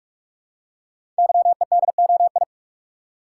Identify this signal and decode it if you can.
Morse code, heard as a keyed tone, about 44 wpm, 700 Hz. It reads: YEDOI
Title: Code Groups - 5 Letters - 45wpm